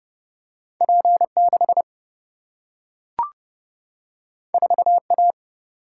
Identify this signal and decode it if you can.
Morse code, heard as a keyed tone, about 30 wpm, 700 Hz. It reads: P6 4A
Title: Single-Letter Single-Number 30wpm